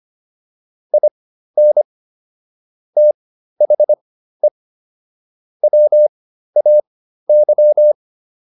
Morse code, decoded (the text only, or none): IN THE WAY